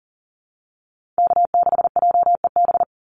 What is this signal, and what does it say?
Morse code, heard as a keyed tone, about 40 wpm, 700 Hz. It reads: K6JEB
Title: Call Signs - 40wpm